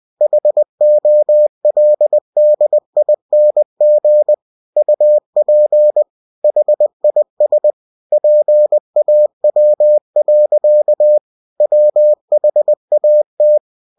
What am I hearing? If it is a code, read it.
Morse code, heard as a keyed tone, about 20 wpm, 600 Hz. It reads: HOLDING UP HIS PAW. WHAT
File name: cw-books / winnie-the-pooh / winnie-20wpm-page-0030